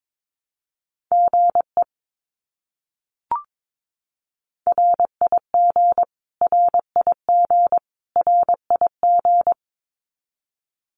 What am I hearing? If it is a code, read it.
Morse code, heard as a keyed tone, about 22 wpm, 700 Hz. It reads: GE RIG RIG RIG
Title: Abbreviations Repeated 3x - 22wpm